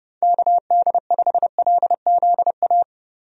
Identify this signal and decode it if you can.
Morse code, heard as a keyed tone, about 32 wpm, 700 Hz. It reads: KD5LZA
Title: USA Call Signs - 6 Characters - Repeated 3x - 30wpm